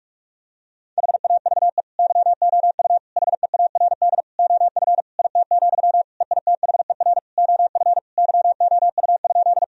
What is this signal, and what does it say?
Morse code, heard as a keyed tone, about 46 wpm, 700 Hz. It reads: HAVE YOU HEARD OF IT, EITHER OF YOU?
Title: Sentences from Top 1000 Words - Review Part 4 - 45wpm